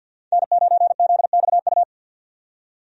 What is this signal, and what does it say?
Morse code, heard as a keyed tone, about 51 wpm, 700 Hz. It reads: N9ZXU
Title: USA Call Signs - 5 Characters - 50wpm